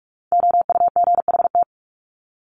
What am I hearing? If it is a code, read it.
Morse code, heard as a keyed tone, about 44 wpm, 700 Hz. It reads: OUGHT
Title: Top 1751-2000 Words 45wpm